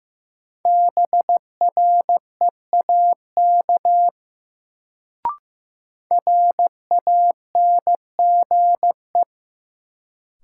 Morse code, heard as a keyed tone, about 15 wpm, 700 Hz. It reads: BREAK RANGE
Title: Sentences from Top 900 Words - New Word Drill - 15wpm